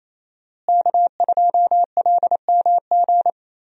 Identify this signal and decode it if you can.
Morse code, heard as a keyed tone, about 27 wpm, 700 Hz. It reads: K2LMG
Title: USA Call Signs - 5 Characters - Repeated 3x - 28wpm